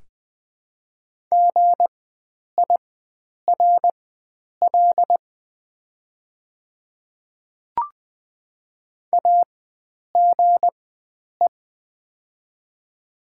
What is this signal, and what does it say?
Morse code, heard as a keyed tone, about 20 wpm, spaced as for 9 wpm, 700 Hz. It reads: GIRL AGE
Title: 4x Character Spacing - Top 500 Words - 20wpm